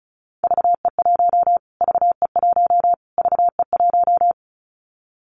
Morse code, heard as a keyed tone, about 34 wpm, 700 Hz. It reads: VE1 VE1 VE1
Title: Callsign Prefixes for Top 25 DXCC - Repeated 3x - 35wpm